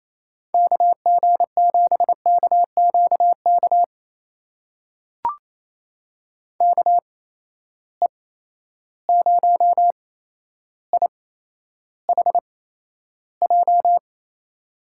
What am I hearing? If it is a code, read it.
Morse code, heard as a keyed tone, about 28 wpm, spaced as for 12 wpm, 700 Hz. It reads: KG7KQK KE0IHJ
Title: USA Call Signs - 6 Characters - 8x Character Spacing - 28wpm